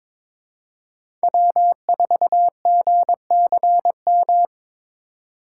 Morse code, heard as a keyed tone, about 23 wpm, 700 Hz. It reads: W4GCM
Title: USA Call Signs - 8x Character Spacing - 22wpm